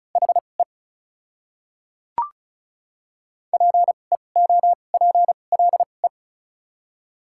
Morse code, heard as a keyed tone, about 36 wpm, spaced as for 25 wpm, 700 Hz. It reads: HE PEOPLE
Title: 2x Character Spacing - Top 100 Words - 35wpm